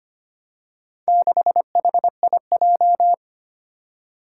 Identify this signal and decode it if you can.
Morse code, heard as a keyed tone, about 25 wpm, 700 Hz. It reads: 6HIJ
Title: Four Challenging Letter-Number Combinations 25wpm